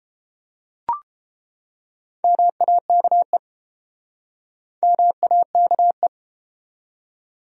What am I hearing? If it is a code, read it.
Morse code, heard as a keyed tone, about 29 wpm, 700 Hz. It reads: MAKE MAKE